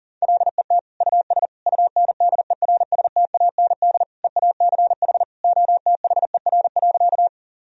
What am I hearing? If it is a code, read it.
Morse code, heard as a keyed tone, about 37 wpm, 700 Hz. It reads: LET US UNDERSTAND EACH OTHER.
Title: Sentences from Top 1000 Words - Review Part 1 - 40wpm